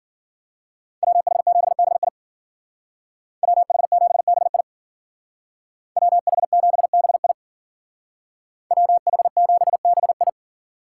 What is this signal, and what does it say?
Morse code, heard as a keyed tone, about 48 wpm, 700 Hz. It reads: WH7BI WH7BI WH7BI WH7BI
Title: Speed Racing - USA 2x2 Call Signs 40wpm